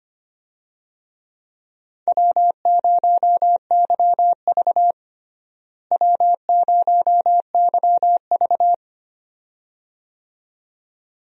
Morse code, heard as a keyed tone, about 25 wpm, 700 Hz. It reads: W0YV W0YV
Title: USA Call Signs - 4 Characters - Repeated 2x - 25wpm